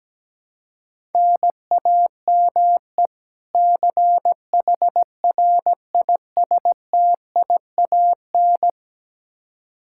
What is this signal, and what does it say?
Morse code, heard as a keyed tone, about 17 wpm, 700 Hz. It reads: NAME CHRISTIAN
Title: Practice QSO - Exchange 1 - Beginning - 17wpm